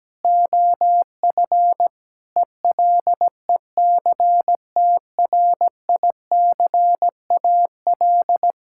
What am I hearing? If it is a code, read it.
Morse code, heard as a keyed tone, about 17 wpm, 700 Hz. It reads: OF ELECTRICAL
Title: Sentences from Top 1000 Words - Review Part 6 - 17wpm